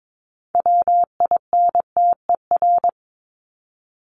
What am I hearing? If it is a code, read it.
Morse code, heard as a keyed tone, about 22 wpm, 700 Hz. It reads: WINTER
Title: Top 1501-1750 Words 22wpm